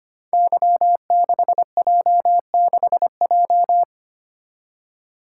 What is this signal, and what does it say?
Morse code, heard as a keyed tone, about 25 wpm, 700 Hz. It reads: Y6J6J
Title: Code Groups - 5 Letter:Numbers - 25wpm